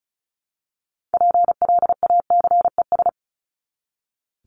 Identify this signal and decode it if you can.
Morse code, heard as a keyed tone, about 34 wpm, 700 Hz. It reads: PLACES
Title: ES:S Suffix - Top 100 Words with Base Words 35wpm